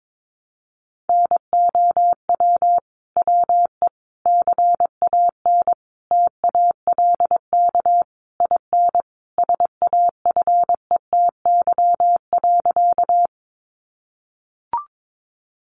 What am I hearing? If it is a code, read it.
Morse code, heard as a keyed tone, about 22 wpm, 700 Hz. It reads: NOW WE CAN TALK IN SAFETY.
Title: Sentences from Top 700 Words 22wpm